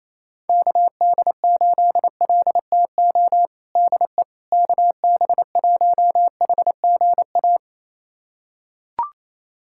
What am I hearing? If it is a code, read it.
Morse code, heard as a keyed tone, about 28 wpm, 700 Hz. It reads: KD8LTO DE KB1HGA
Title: USA Callsign DE Callsign - 6 Characters - 28wpm